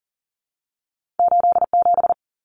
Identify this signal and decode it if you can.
Morse code, heard as a keyed tone, about 40 wpm, 700 Hz. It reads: 87
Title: Common Cut Numbers - 2-Digit 40wpm